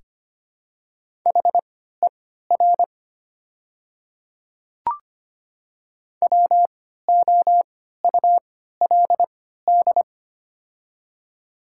Morse code, heard as a keyed tone, about 25 wpm, spaced as for 14 wpm, 700 Hz. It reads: HER WOULD
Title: 3x Character Spacing - Top 300 Words - 25wpm